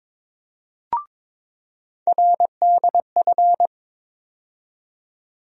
Morse code, heard as a keyed tone, about 21 wpm, 700 Hz. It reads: RDF
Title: USA Call Sign Postfixes - 3 Letters 22wpm